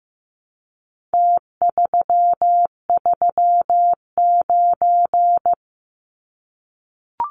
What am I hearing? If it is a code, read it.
Morse code, heard as a keyed tone, about 15 wpm, 700 Hz. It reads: T339
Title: Numbers - Contest Style - 4-Digit 15wpm